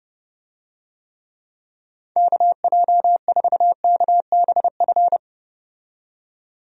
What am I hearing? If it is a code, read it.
Morse code, heard as a keyed tone, about 31 wpm, 700 Hz. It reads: KJ4KBF
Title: USA Call Signs - 6 Characters - 30wpm